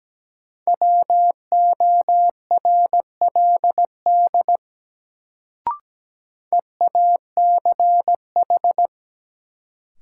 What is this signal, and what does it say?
Morse code, heard as a keyed tone, about 17 wpm, 700 Hz. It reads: WORLD EACH
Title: Sentences from Top 200 Words - New Word Drill - 17wpm